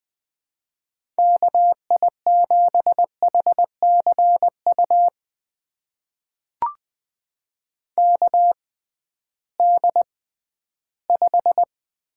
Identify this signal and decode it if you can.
Morse code, heard as a keyed tone, about 20 wpm, spaced as for 13 wpm, 700 Hz. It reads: KI7HCU KD5
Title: USA Call Signs - 6 Characters - 6x Character Spacing - 20wpm